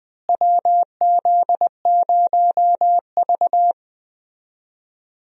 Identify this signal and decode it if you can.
Morse code, heard as a keyed tone, about 20 wpm, 700 Hz. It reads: WZ0V
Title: USA Call Signs - 4 Characters - Repeated 3x - 20wpm